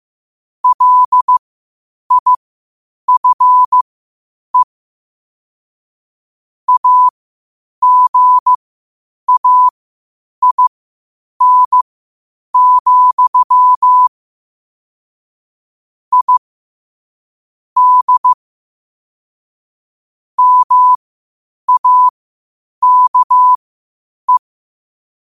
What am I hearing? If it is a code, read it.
Morse code, heard as a keyed tone, about 15 wpm, spaced as for 8 wpm, 1000 Hz. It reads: LIFE AGAIN, I D MAKE
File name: qotdfiles / 8wpm